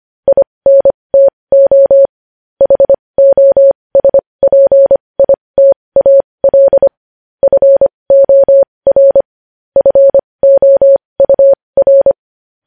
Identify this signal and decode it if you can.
Morse code, heard as a keyed tone, about 25 wpm, spaced as for 20 wpm, 550 Hz. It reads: INTO HOSPITAL FOR FOUR
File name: podcast / News headlines-1x-25x20